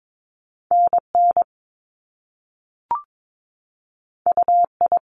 Morse code, heard as a keyed tone, about 22 wpm, 700 Hz. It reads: NN UI
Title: Top Two Letter Combinations in Words 22wpm